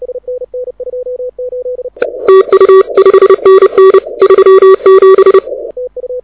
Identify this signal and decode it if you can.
Morse code, heard as a keyed tone, about 30 wpm, 380 Hz. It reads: TU 5NN 37
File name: audio zawody / IARU 2015 CW SN7Q